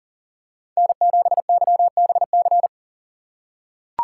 Morse code, heard as a keyed tone, about 40 wpm, 700 Hz. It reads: N7YBC
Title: USA Call Signs - 5 Characters - 2x Character Spacing - 40wpm